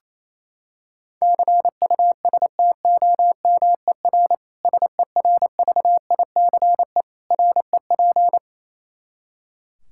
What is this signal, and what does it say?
Morse code, heard as a keyed tone, about 28 wpm, 700 Hz. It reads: CUSTOMER SERVICE REP
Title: Practice QSO - Exchange 3 - More Info - 28wpm